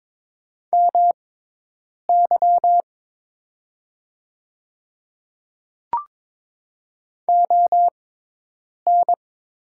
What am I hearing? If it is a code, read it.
Morse code, heard as a keyed tone, about 22 wpm, spaced as for 8 wpm, 700 Hz. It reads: MY ON
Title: 6x Character Spacing - Top 200 Words - 22wpm